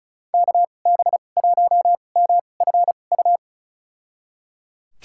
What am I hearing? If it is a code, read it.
Morse code, heard as a keyed tone, about 34 wpm, spaced as for 25 wpm, 700 Hz. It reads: KB1MFU
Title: USA Call Signs - 6 Characters - 2x Character Spacing - 35wpm